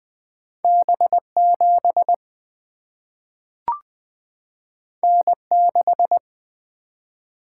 Morse code, heard as a keyed tone, about 20 wpm, 700 Hz. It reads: B 7 N 6
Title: Single-Letter Single-Number 20wpm